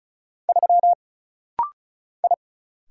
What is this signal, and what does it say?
Morse code, heard as a keyed tone, about 34 wpm, 700 Hz. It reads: 3 I
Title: Single Letter-Number - Rapid-Fire 35wpm